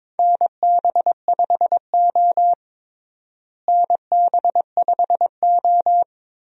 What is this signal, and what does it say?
Morse code, heard as a keyed tone, about 22 wpm, 700 Hz. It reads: NB5O NB5O
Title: USA Call Signs - 4 Characters - Repeated 3x - 22wpm